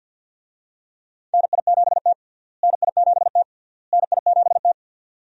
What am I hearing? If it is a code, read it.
Morse code, heard as a keyed tone, about 51 wpm, 700 Hz. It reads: NI7T NI7T NI7T
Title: USA Call Signs - 4 Characters - Repeated 3x - 50wpm